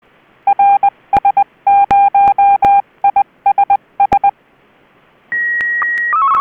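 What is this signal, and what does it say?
Morse code, heard as a keyed tone, about 20 wpm, 800 Hz. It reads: RS0ISS
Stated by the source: SSTV Martin 1 transmission from the ISS, RS0ISS, received on 145.800 MHz during a pass, by OK2ZIL & OK2ZBO.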